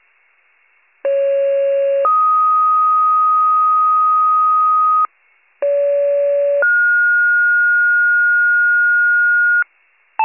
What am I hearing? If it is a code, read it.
Two-tone page, heard as a one-second tone, then a three-second tone, about 550 then 1250 Hz, about 570 then 1430 Hz, with noise. It is heard from 1.0 to 9.6 s.